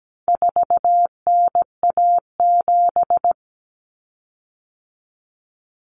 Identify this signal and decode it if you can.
Morse code, heard as a keyed tone, about 17 wpm, 700 Hz. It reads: 4NA7
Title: Common Cut Numbers - 4-Digit 17wpm